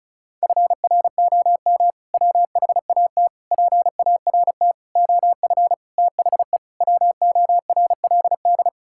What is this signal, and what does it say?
Morse code, heard as a keyed tone, about 36 wpm, 700 Hz. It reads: FROM WHAT PART OF THE WORLD
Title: Sentences from Top 1000 Words - Review Part 6 - 35wpm